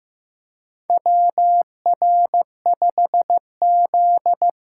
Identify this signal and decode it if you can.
Morse code, heard as a keyed tone, about 15 wpm, 700 Hz. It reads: WR5Z
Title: USA Call Signs - 4 Characters - 2x Character Spacing - 15wpm